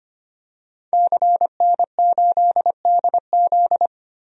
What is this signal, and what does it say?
Morse code, heard as a keyed tone, about 25 wpm, 700 Hz. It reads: CN8DZ
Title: DXCC-100 Call Signs - 5 Characters - 25wpm